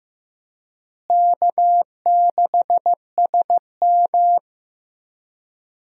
Morse code, heard as a keyed tone, about 15 wpm, 700 Hz. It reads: K6SM
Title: USA Call Signs - 4 Characters - Repeated 2x - 15wpm